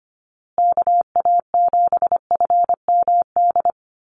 Morse code, heard as a keyed tone, about 25 wpm, 700 Hz. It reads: KA7FMD
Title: USA Call Signs Repeated 2x 25wpm